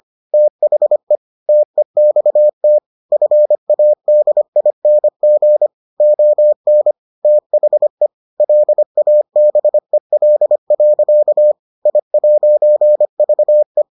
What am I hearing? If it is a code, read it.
Morse code, heard as a keyed tone, about 25 wpm, 600 Hz. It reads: THE TEXT FADING ON THE LABEL. I'VE